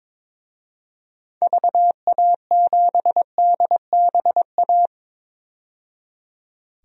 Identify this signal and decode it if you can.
Morse code, heard as a keyed tone, about 21 wpm, 700 Hz. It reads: VA7DBA
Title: DXCC-100 Call Signs - 6 Characters - 22wpm